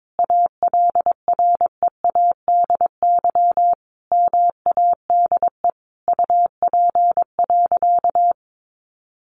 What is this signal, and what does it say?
Morse code, heard as a keyed tone, about 22 wpm, 700 Hz. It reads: ALREADY MADE UP.
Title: Sentences from Top 700 Words 22wpm